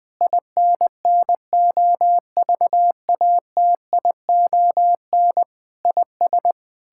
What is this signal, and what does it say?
Morse code, heard as a keyed tone, about 20 wpm, 700 Hz. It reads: INNOVATION IS
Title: Difficult Sentences with Hints - 20wpm